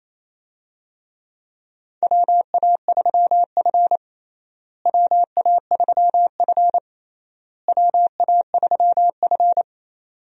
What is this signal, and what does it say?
Morse code, heard as a keyed tone, about 28 wpm, 700 Hz. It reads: WA3F WA3F WA3F
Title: USA Call Signs - 4 Characters - Repeated 3x - 28wpm